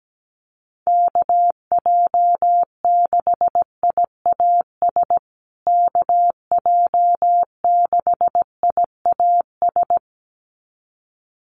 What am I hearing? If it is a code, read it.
Morse code, heard as a keyed tone, about 17 wpm, 700 Hz. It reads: KJ6IAS KJ6IAS
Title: USA Call Signs Repeated 2x 17wpm